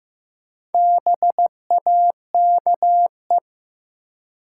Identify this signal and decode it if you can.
Morse code, heard as a keyed tone, about 15 wpm, 700 Hz. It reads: BAKE
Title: Four Letter Words - 15wpm